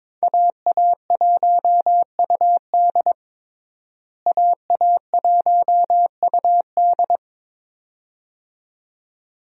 Morse code, heard as a keyed tone, about 22 wpm, 700 Hz. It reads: AA1UD AA1UD
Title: USA Call Signs - 5 Characters - Repeated 2x - 22wpm